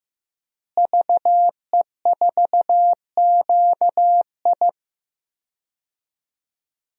Morse code, heard as a keyed tone, about 15 wpm, 700 Hz. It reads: VE4QI
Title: DXCC-100 Call Signs - 5 Characters - Repeated 3x - 15wpm